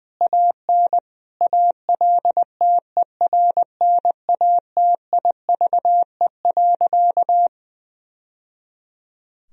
Morse code, heard as a keyed tone, about 20 wpm, 700 Hz. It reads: AN ALTERNATIVE.
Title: Sentences from Top 1000 Words - Review Part 4 - 20wpm